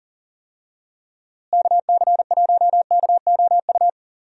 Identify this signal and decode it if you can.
Morse code, heard as a keyed tone, about 40 wpm, 700 Hz. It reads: KC1KOU
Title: USA Call Signs - 6 Characters - Repeated 3x - 40wpm